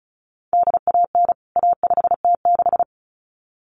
Morse code, heard as a keyed tone, about 34 wpm, 700 Hz. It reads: DAN A5T6
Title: CWT Exchange 35wpm